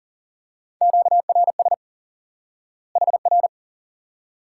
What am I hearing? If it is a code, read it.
Morse code, heard as a keyed tone, about 40 wpm, 700 Hz. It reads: QRS HR
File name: IQE Course 3 - Lesson 89 - YAGI - Two Elements - 40wpm